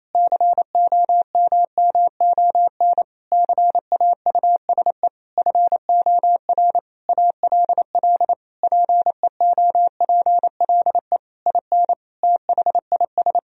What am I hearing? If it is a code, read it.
Morse code, heard as a keyed tone, about 28 wpm, 700 Hz. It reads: COMMON CAUSE FOR ALL PEOPLE IN THIS
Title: Sentences from Top 1000 Words - Review Part 3 - 28wpm